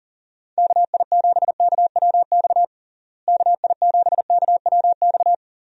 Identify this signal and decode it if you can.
Morse code, heard as a keyed tone, about 40 wpm, 700 Hz. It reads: KI7KWX KI7KWX
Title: USA Call Signs - 6 Characters - Repeated 3x - 40wpm